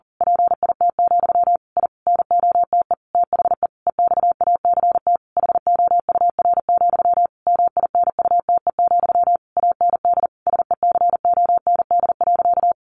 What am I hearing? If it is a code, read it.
Morse code, heard as a keyed tone, about 40 wpm, 700 Hz. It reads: PIT, I NOTE THE EXACT HOUR, MINUTE, AND SECOND.
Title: A Sound Of Thunder 40wpm